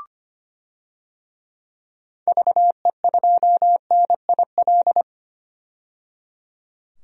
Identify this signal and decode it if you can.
Morse code, heard as a keyed tone, about 25 wpm, 700 Hz. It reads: VE2NIL
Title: DXCC-100 Call Signs - 6 Characters - 25wpm